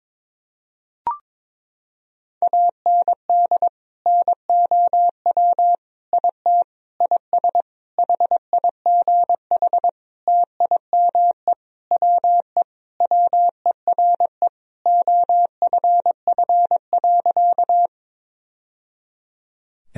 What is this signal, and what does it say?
Morse code, heard as a keyed tone, about 22 wpm, 700 Hz. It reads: AND NOW IT IS HIGH TIME WE WERE OFF.
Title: Sentences from Top 1000 Words - Review Part 1 - 22wpm